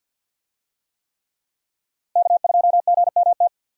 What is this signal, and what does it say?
Morse code, heard as a keyed tone, about 50 wpm, 700 Hz. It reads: K2GMT
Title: USA Call Signs - 5 Characters - Repeated 3x - 50wpm